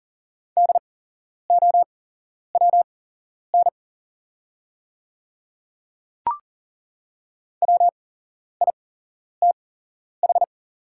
Morse code, heard as a keyed tone, about 41 wpm, spaced as for 11 wpm, 700 Hz. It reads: DOWN WITH
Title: 8x Character Spacing - Top 100 Words - 40wpm